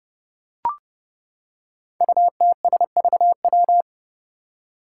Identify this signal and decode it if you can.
Morse code, heard as a keyed tone, about 31 wpm, 700 Hz. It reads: UTSVW
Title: Five Challenging Letter-Number Combinations 30wpm